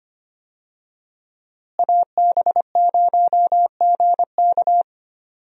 Morse code, heard as a keyed tone, about 25 wpm, 700 Hz. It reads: AB0GK
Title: USA Call Signs - 5 Characters - Repeated 2x - 25wpm